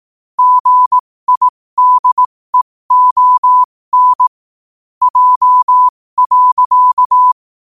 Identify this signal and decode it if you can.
Morse code, heard as a keyed tone, about 18 wpm, spaced as for 15 wpm, 1000 Hz. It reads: GIDEON J.
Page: 15 WPM morse code quotes